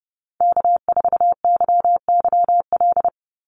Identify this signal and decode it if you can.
Morse code, heard as a keyed tone, about 30 wpm, 700 Hz. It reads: K4YYL
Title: Call Signs Repeated 2x - 30wpm